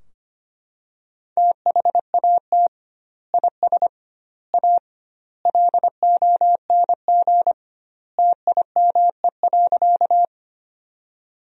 Morse code, heard as a keyed tone, about 25 wpm, 700 Hz. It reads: THAT IS A LONG TIME.
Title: 2x Word-Spacing - Sentences from Top 100 Words - 25wpm